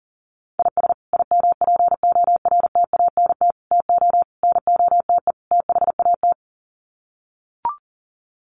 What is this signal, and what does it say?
Morse code, heard as a keyed tone, about 40 wpm, 700 Hz. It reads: IS IMPORTANT TO NOTE THAT
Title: Sets of 5 Words 40wpm